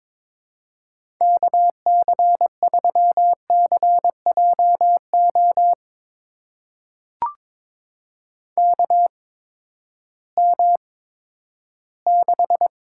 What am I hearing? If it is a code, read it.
Morse code, heard as a keyed tone, about 22 wpm, spaced as for 13 wpm, 700 Hz. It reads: KC3CJO KM6
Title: USA Call Signs - 6 Characters - 8x Character Spacing - 22wpm